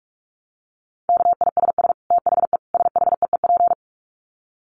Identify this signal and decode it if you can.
Morse code, heard as a keyed tone, about 44 wpm, 700 Hz. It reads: KISS THE SHEEP
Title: Dit Challenge 45wpm